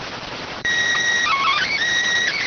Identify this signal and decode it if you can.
SSTV (slow-scan television), PD 290.